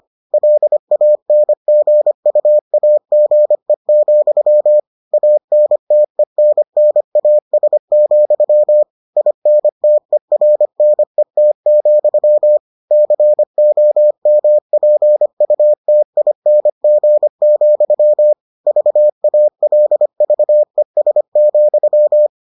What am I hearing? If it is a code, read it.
Morse code, heard as a keyed tone, about 25 wpm, 600 Hz. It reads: LANGUAGE, ANTENNAS, INTERNET, COMPUTING, VALVES,